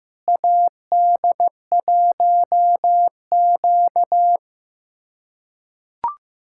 Morse code, heard as a keyed tone, about 16 wpm, 700 Hz. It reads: AD1Q
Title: USA Callsign DE Callsign - 4 Characters - 15wpm